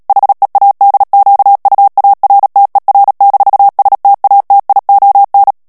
Morse code, heard as a keyed tone, about 37 wpm, 800 Hz. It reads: HEADQUARTER-STATION
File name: morse mp3 / ALLEMAND texte clair 2 vitesse 35 mots minute